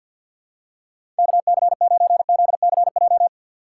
Morse code, heard as a keyed tone, about 53 wpm, 700 Hz. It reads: KC9ZCJ
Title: USA Call Signs - 6 Characters - 6x Character Spacing - 50wpm